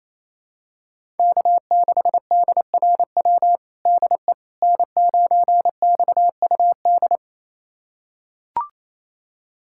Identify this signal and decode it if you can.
Morse code, heard as a keyed tone, about 28 wpm, 700 Hz. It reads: K6DRW DE N9XUD
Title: USA Callsign DE Callsign - 5 Characters - 28wpm